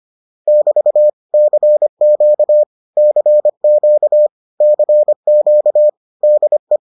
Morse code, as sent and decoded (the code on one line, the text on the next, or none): -...- / -.-. --.- / -.-. --.- / -.-. --.- / -.. .
<BT> CQ CQ CQ DE